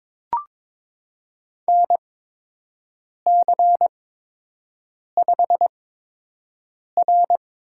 Morse code, heard as a keyed tone, about 22 wpm, spaced as for 6 wpm, 700 Hz. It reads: NC5R
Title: USA Call Signs - 4 Characters - 8x Character Spacing - 22wpm